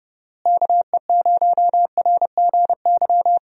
Morse code, heard as a keyed tone, about 32 wpm, 700 Hz. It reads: KE0RGY
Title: USA Call Signs - 6 Characters - Repeated 2x - 30wpm